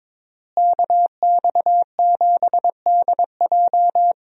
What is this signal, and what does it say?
Morse code, heard as a keyed tone, about 21 wpm, 700 Hz. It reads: KX7DJ
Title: USA Call Signs - 5 Characters - Repeated 3x - 22wpm